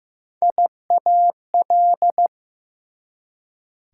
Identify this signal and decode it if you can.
Morse code, heard as a keyed tone, about 15 wpm, 700 Hz. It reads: IAL
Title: Prefixes and Suffixes 15wpm